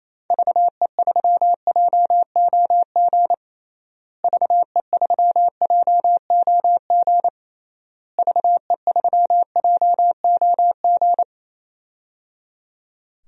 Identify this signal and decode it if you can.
Morse code, heard as a keyed tone, about 28 wpm, 700 Hz. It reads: VE3JOG VE3JOG VE3JOG
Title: DXCC-100 Call Signs - 6 Characters - Repeated 3x - 28wpm